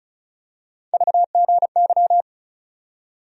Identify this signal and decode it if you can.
Morse code, heard as a keyed tone, about 35 wpm, 700 Hz. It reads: VGY
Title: USA Call Sign Postfixes - 3 Letters 35wpm